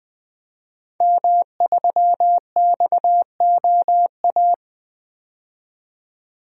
Morse code, heard as a keyed tone, about 20 wpm, 700 Hz. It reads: M3XOA
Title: DXCC-100 Call Signs - 5 Characters - 20wpm